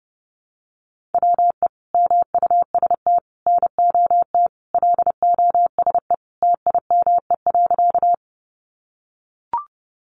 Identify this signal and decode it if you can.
Morse code, heard as a keyed tone, about 30 wpm, 700 Hz. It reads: WE MUST NOT LOSE TIME.
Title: Sentences from Top 1100 Words 30wpm